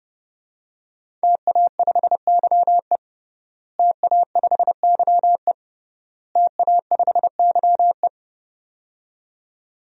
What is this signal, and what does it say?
Morse code, heard as a keyed tone, about 32 wpm, 700 Hz. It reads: TA5YE TA5YE TA5YE
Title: DXCC-100 Call Signs - 5 Characters - Repeated 3x - 30wpm